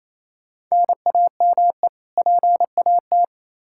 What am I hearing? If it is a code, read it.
Morse code, heard as a keyed tone, about 28 wpm, 700 Hz. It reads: NAME PAT
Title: Practice QSO - Exchange 1 - Beginning - 28wpm